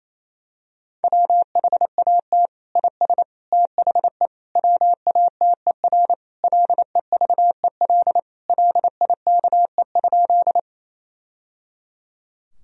Morse code, heard as a keyed tone, about 28 wpm, 700 Hz. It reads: WHAT IS THE WATER LEVEL LIKE?
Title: Sentences from Top 1000 Words - Review Part 4 - 28wpm